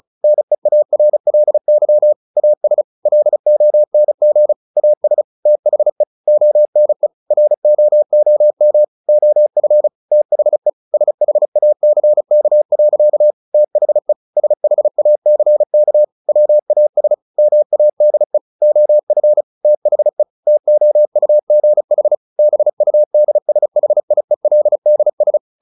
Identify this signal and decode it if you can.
Morse code, heard as a keyed tone, about 34 wpm, 600 Hz. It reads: NEARLY AS LONG AS THE ONE ROOM OF THE SHACK. THE SHACK WAS MADE OF THE TOUGH BUDSHIELDS
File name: the old man and the sea / the old man and the sea 1 35wpm